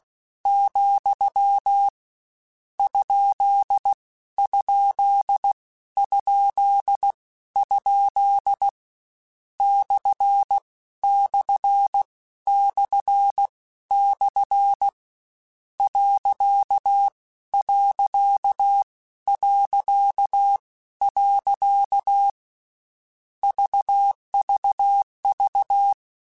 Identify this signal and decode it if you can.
Morse code, heard as a keyed tone, about 16 wpm, spaced as for 13 wpm, 800 Hz. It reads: , ? ? ? ? / // / .. .. VVV
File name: Misc / Misc-mp3s / Punct @15